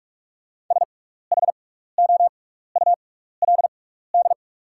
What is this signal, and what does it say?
Morse code, heard as a keyed tone, about 44 wpm, spaced as for 15 wpm, 700 Hz. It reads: SHOULD
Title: 6x Character Spacing - Top 300 Words - 45wpm